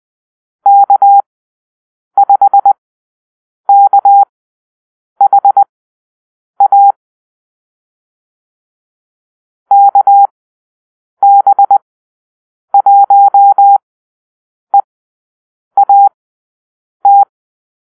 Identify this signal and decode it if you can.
Morse code, heard as a keyed tone, about 20 wpm, spaced as for 7 wpm, 800 Hz. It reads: K5KHA KB1EAT
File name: arrl / Call-Sign Drill-10